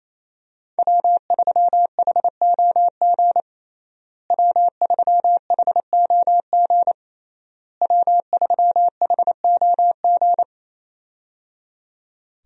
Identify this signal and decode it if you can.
Morse code, heard as a keyed tone, about 27 wpm, 700 Hz. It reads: W3HOG W3HOG W3HOG
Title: USA Call Signs - 5 Characters - Repeated 3x - 28wpm